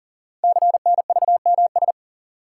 Morse code, heard as a keyed tone, about 38 wpm, 690 Hz. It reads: CNVMS
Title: Code Groups - 5 Letters - 40wpm